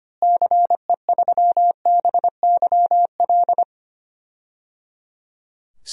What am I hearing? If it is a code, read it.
Morse code, heard as a keyed tone, about 25 wpm, 700 Hz. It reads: CE3BYL
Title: DXCC-100 Call Signs - 6 Characters - Repeated 3x - 25wpm